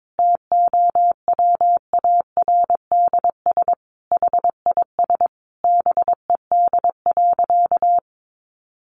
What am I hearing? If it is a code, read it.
Morse code, heard as a keyed tone, about 22 wpm, 700 Hz. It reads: TOWARDS HIS BED.
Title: Sentences from Top 1100 Words 22wpm